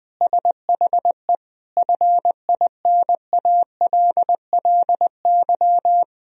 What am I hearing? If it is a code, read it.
Morse code, heard as a keyed tone, about 20 wpm, 700 Hz. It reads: SHE FINALLY
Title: Sentences from Top 1000 Words - Review Part 3 - 20wpm